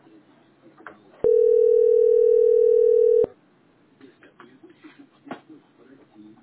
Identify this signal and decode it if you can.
Ringback tone (North America), 440 + 480 Hz.